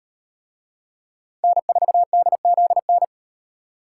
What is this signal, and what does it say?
Morse code, heard as a keyed tone, about 36 wpm, 700 Hz. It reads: N4DZN